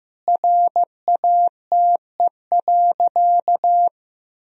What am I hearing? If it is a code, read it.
Morse code, heard as a keyed tone, about 15 wpm, 700 Hz. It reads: RATE.
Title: Sentences from Top 1000 Words - Review Part 4 - 15wpm